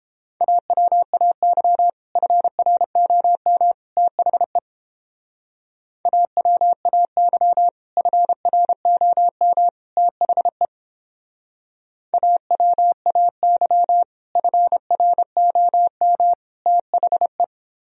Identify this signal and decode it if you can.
Morse code, heard as a keyed tone, about 29 wpm, 700 Hz. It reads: AWAY FROM THE AWAY FROM THE AWAY FROM THE